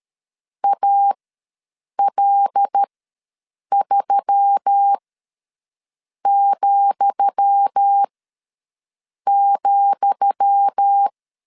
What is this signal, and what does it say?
Morse code, heard as a keyed tone, about 13 wpm, spaced as for 6 wpm, 800 Hz. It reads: AL3,,
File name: LEZIONI / 10 SIGNS / 10D COMA ws 12wpm 3618